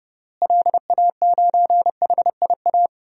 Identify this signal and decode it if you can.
Morse code, heard as a keyed tone, about 32 wpm, 700 Hz. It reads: LA9HIA
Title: DXCC-100 Call Signs - 6 Characters - Repeated 3x - 30wpm